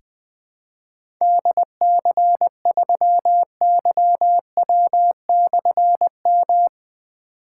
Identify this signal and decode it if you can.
Morse code, heard as a keyed tone, about 21 wpm, 700 Hz. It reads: DC3YW/M
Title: Difficult Call Signs 20wpm